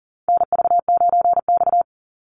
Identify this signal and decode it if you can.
Morse code, heard as a keyed tone, about 40 wpm, 700 Hz. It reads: NV9X
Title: Call Signs Repeated 3x - 40wpm